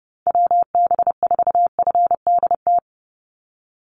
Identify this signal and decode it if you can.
Morse code, heard as a keyed tone, about 30 wpm, 700 Hz. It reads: WB4FDT
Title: Call Signs Repeated 2x - 30wpm